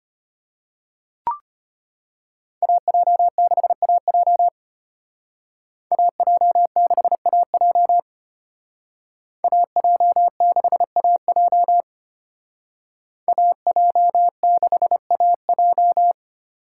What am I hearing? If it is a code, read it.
Morse code, heard as a keyed tone, about 32 wpm, 700 Hz. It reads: AJ6AJ AJ6AJ AJ6AJ AJ6AJ